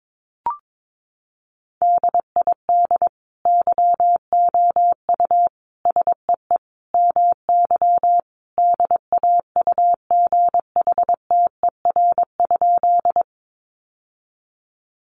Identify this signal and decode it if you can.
Morse code, heard as a keyed tone, about 22 wpm, 700 Hz. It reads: DID YOU SEE MY DAUGHTER?
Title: Sentences from Top 1500 Words 22wpm